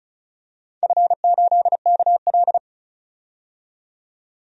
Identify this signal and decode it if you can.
Morse code, heard as a keyed tone, about 36 wpm, 700 Hz. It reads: F8KL
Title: DXCC-100 Call Signs - 4 Characters - 35wpm